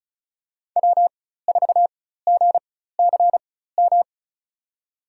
Morse code, heard as a keyed tone, about 36 wpm, spaced as for 16 wpm, 700 Hz. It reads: W4GCM
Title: USA Call Signs - 5 Characters - 4x Character Spacing - 35wpm